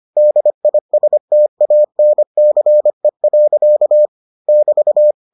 Morse code, heard as a keyed tone, about 25 wpm, 600 Hz. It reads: DISTANCE. <BT>